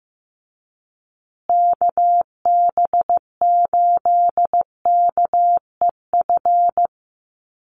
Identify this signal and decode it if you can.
Morse code, heard as a keyed tone, about 15 wpm, 700 Hz. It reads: KB8KEF
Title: USA Call Signs Repeated 3x 15wpm